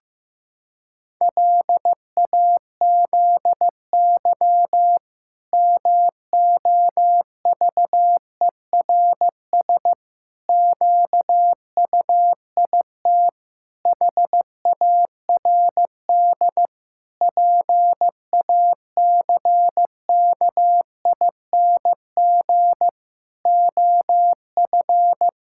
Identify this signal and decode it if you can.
Morse code, heard as a keyed tone, about 15 wpm, 700 Hz. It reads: LAZY MOVERS QUIT HARD PACKING OF
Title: Pangram Challenge v2 - 15wpm